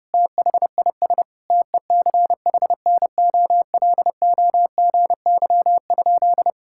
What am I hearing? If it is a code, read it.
Morse code, heard as a keyed tone, about 32 wpm, 700 Hz. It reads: THIS TECHNOLOGY?
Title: Sentences from Top 1000 Words - Review Part 5 - 30wpm